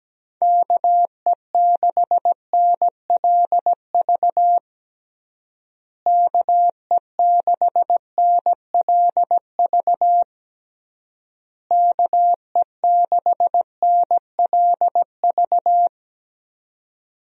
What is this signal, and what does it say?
Morse code, heard as a keyed tone, about 17 wpm, 700 Hz. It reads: KE6NLV KE6NLV KE6NLV
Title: USA Call Signs - 6 Characters - Repeated 3x - 17wpm